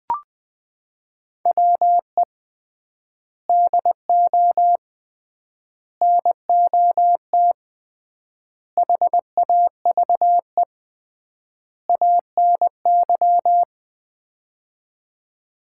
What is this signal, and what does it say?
Morse code, heard as a keyed tone, about 20 wpm, 700 Hz. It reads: WE DO NOT HAVE ANY
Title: 3x Word-Spacing - Sets of 5 Words - 20wpm